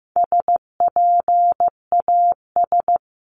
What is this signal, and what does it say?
Morse code, heard as a keyed tone, about 15 wpm, 700 Hz. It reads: SPAS
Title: CWT Exchange 15wpm